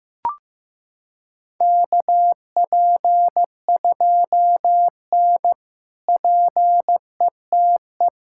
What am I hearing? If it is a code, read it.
Morse code, heard as a keyed tone, about 15 wpm, 700 Hz. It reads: KP2N PETE
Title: K1USN SST QSO Practice - 15wpm